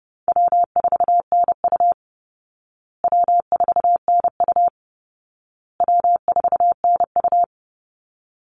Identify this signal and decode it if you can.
Morse code, heard as a keyed tone, about 30 wpm, 700 Hz. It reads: W4NU W4NU W4NU
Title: Easy Call Signs Repeated 3x - 30wpm